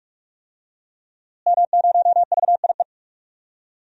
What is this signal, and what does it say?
Morse code, heard as a keyed tone, about 41 wpm, 700 Hz. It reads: M0VIE
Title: DXCC-100 Call Signs - 5 Characters - Repeated 2x - 45wpm